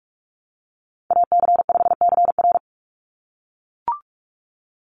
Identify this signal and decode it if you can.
Morse code, heard as a keyed tone, about 44 wpm, 700 Hz. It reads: AC5CR
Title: USA Call Signs 45wpm